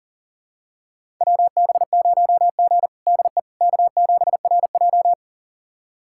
Morse code, heard as a keyed tone, about 40 wpm, 700 Hz. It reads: WB0G DE K7RJ
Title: USA Callsign DE Callsign - 4 Characters - 40wpm